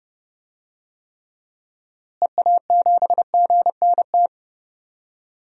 Morse code, heard as a keyed tone, about 31 wpm, 700 Hz. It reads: EA7GNT
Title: DXCC-100 Call Signs - 6 Characters - Repeated 3x - 30wpm